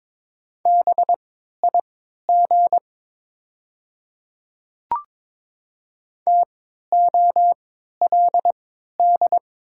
Morse code, heard as a keyed tone, about 22 wpm, spaced as for 13 wpm, 700 Hz. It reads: BIG TOLD
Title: 3x Character Spacing - Top 500 Words - 22wpm